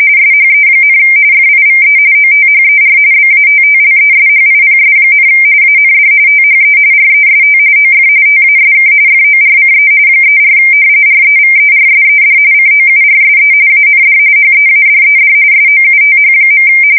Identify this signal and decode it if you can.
RTTY, about 45.45 baud. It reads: SHE, FIVE OTHER BATTLESHIPS, AND THE REST OF TASK FORCE 34 REVERSED COURSE AND HEADED SOUTH.
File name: fsk170 / battleshipnj